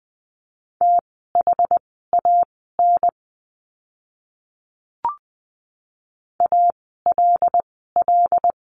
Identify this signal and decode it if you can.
Morse code, heard as a keyed tone, about 19 wpm, spaced as for 14 wpm, 700 Hz. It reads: THAN ALL
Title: 146 Lesson 24f - Letter y - Words 20wpm